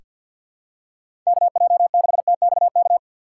Morse code, heard as a keyed tone, about 48 wpm, 700 Hz. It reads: KJ6TXK
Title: USA Call Signs - 6 Characters - 6x Character Spacing - 50wpm